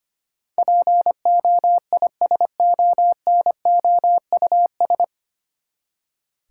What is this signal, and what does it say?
Morse code, heard as a keyed tone, about 25 wpm, 700 Hz. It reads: POISONOUS
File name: Nine Letter Words - 25wpm